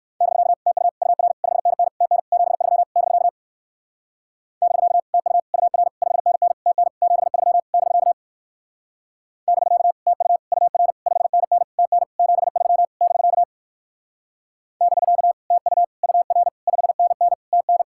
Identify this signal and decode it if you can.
Morse code, heard as a keyed tone, about 56 wpm, 700 Hz. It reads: <BK> TU UR 5NN TN 73 <BK> <BK> TU UR 5NN TN 73 <BK> <BK> TU UR 5NN TN 73 <BK> <BK> TU UR 5NN TN
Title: Speed-Racing Common QSO Phrases - 45wpm